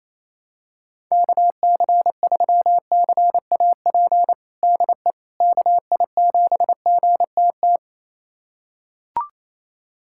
Morse code, heard as a keyed tone, about 28 wpm, 700 Hz. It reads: KC3CAP DE KI7GTT
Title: USA Callsign DE Callsign 28wpm